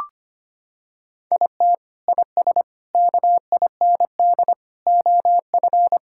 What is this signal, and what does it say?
Morse code, heard as a keyed tone, about 25 wpm, 700 Hz. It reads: IT IS KIND OF
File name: practice / 25wpm / Sentences with 2-Word Phrases - 25wpm